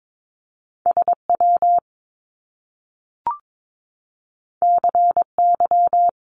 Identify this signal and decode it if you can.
Morse code, heard as a keyed tone, about 22 wpm, 700 Hz. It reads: SW CY
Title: Top Two Letter Combinations in Words 22wpm